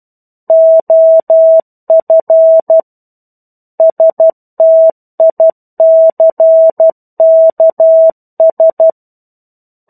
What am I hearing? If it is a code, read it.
Morse code, heard as a keyed tone, about 12 wpm, 650 Hz. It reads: OF STICKS
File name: 01-Chapter-01 12 WPM